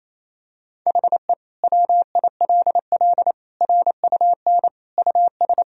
Morse code, heard as a keyed tone, about 28 wpm, 700 Hz. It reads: HE WILL RUN US
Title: Sentences from Top 1000 Words - Review Part 1 - 28wpm